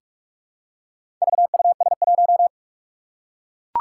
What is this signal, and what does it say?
Morse code, heard as a keyed tone, about 44 wpm, 700 Hz. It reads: VUS1
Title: Four Challenging Letter-Number Combinations 45wpm